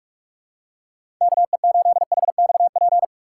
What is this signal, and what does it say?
Morse code, heard as a keyed tone, about 46 wpm, 700 Hz. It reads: KE8HXP
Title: USA Call Signs - 2x Character Spacing - 45wpm